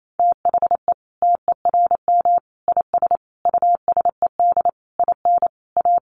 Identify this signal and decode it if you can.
Morse code, heard as a keyed tone, about 28 wpm, 700 Hz. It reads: THE TERM IS USED IN A
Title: Sentences from Top 900 Words 28wpm